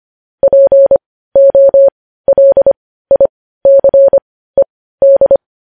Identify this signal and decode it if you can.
Morse code, heard as a keyed tone, about 25 wpm, spaced as for 15 wpm, 550 Hz. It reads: POLICED
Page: News headlines at 15